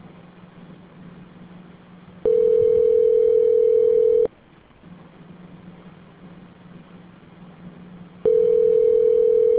Ringback tone (North America), 440 + 480 Hz.